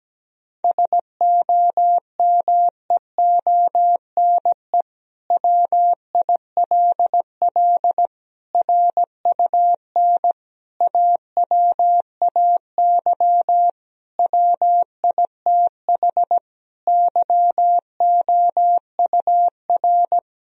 Morse code, heard as a keyed tone, about 17 wpm, 700 Hz. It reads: SOMEONE WILL RUN AWAY WITH YOUR
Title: Sentences from Top 1000 Words - Review Part 1 - 17wpm